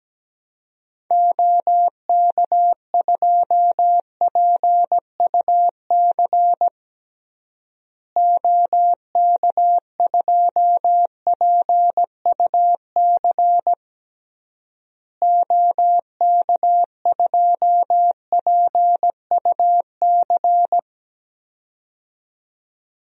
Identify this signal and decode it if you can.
Morse code, heard as a keyed tone, about 17 wpm, 700 Hz. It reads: OK2PUC OK2PUC OK2PUC
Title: DXCC-100 Call Signs - 6 Characters - Repeated 3x - 17wpm